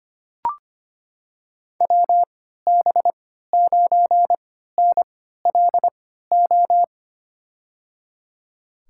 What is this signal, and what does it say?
Morse code, heard as a keyed tone, about 25 wpm, spaced as for 14 wpm, 700 Hz. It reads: WB9NLO
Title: USA Call Signs - 6 Characters - 3x Character Spacing - 25wpm